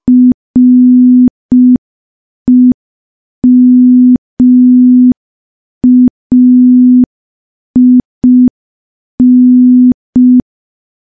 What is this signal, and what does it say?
Morse code, heard as a keyed tone, about 5 wpm, 250 Hz. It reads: REMAIN
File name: morse / morse-training-05WPM